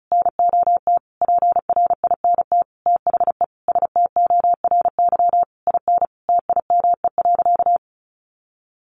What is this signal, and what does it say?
Morse code, heard as a keyed tone, about 34 wpm, 700 Hz. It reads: NOT PRINT THE STORY IN TIME.
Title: Sentences from Top 1400 Words 35wpm